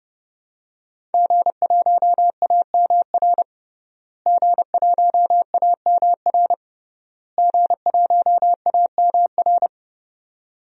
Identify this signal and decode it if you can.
Morse code, heard as a keyed tone, about 30 wpm, 700 Hz. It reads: G1AMR G1AMR G1AMR
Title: DXCC-100 Call Signs - 5 Characters - Repeated 3x - 30wpm